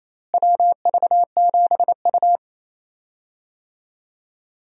Morse code, heard as a keyed tone, about 28 wpm, 700 Hz. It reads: WV7U
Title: USA Call Signs - 4 Characters - Repeated 3x - 28wpm